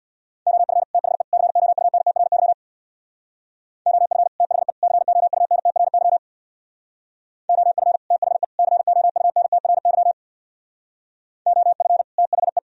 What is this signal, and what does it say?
Morse code, heard as a keyed tone, about 63 wpm, 700 Hz. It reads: OF THE COUNTRY OF THE COUNTRY OF THE COUNTRY OF THE